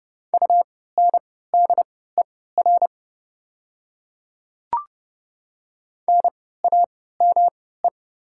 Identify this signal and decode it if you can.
Morse code, heard as a keyed tone, about 30 wpm, spaced as for 17 wpm, 700 Hz. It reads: UNDER NAME
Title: 3x Character Spacing - Top 500 Words - 30wpm